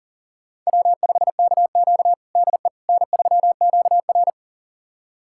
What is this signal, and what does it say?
Morse code, heard as a keyed tone, about 40 wpm, 700 Hz. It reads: W5KQ DE N3QR
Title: USA Callsign DE Callsign - 4 Characters - 40wpm